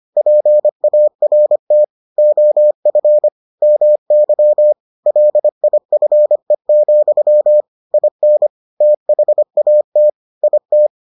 Morse code, heard as a keyed tone, about 25 wpm, 600 Hz. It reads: PART OF MY LIFE, IN THAT IT